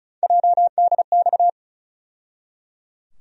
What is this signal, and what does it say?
Morse code, heard as a keyed tone, about 34 wpm, 700 Hz. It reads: JDX
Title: USA Call Sign Postfixes - 3 Letters 35wpm